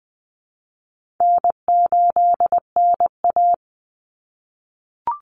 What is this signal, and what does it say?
Morse code, heard as a keyed tone, about 20 wpm, 700 Hz. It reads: N8NA
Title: Easy Call Signs - 20wpm